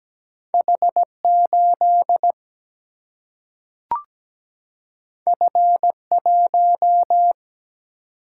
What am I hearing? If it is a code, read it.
Morse code, heard as a keyed tone, about 17 wpm, 700 Hz. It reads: H8 F1
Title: Single-Letter Single-Number 17wpm